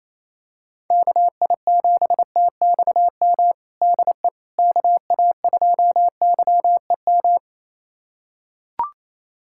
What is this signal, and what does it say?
Morse code, heard as a keyed tone, about 28 wpm, 700 Hz. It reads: KI7TXM DE KA2YEM
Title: USA Callsign DE Callsign - 6 Characters - 28wpm